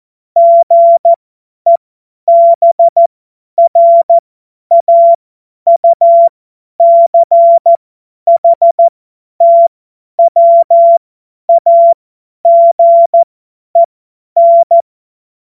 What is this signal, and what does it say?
Morse code, heard as a keyed tone, about 14 wpm, spaced as for 10 wpm, 680 Hz. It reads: GEBRAUCHTWAGEN
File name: Morsekurs / Witze / Witz 08. August 2023 50 F70